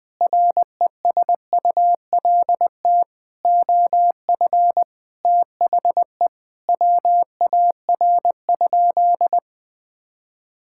Morse code, heard as a keyed tone, about 20 wpm, 700 Hz. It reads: RESULT OF THE WAR?
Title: Sentences from Top 1000 Words - Review Part 5 - 20wpm